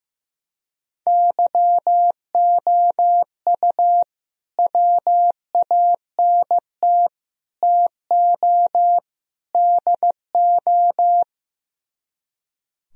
Morse code, heard as a keyed tone, about 16 wpm, 700 Hz. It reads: YOU WANT TO DO
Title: Speed-Racing - Sets of 4 Words 15wpm